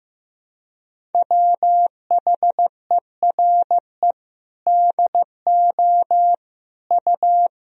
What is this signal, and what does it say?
Morse code, heard as a keyed tone, about 15 wpm, 700 Hz. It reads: WHERE DO U
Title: Sentences from Top 200 Words and Abbreviations 15wpm